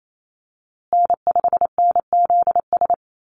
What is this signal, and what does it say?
Morse code, heard as a keyed tone, about 28 wpm, 700 Hz. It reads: N5NZS
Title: USA Call Signs Repeated 3x 28wpm